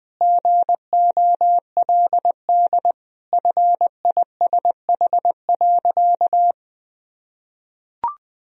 Morse code, heard as a keyed tone, about 20 wpm, 700 Hz. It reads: GOLD FISH.
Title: Sentences from Top 1000 Words - Review Part 3 - 20wpm